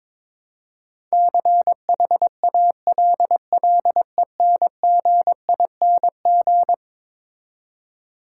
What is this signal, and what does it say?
Morse code, heard as a keyed tone, about 21 wpm, 700 Hz. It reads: CHALLENGING
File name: practice / 22wpm / Eleven Letter Words - 22wpm